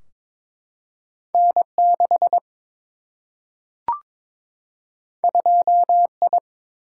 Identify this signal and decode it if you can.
Morse code, heard as a keyed tone, about 22 wpm, 700 Hz. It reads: N6 2I
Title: Single-Letter Single-Number 22wpm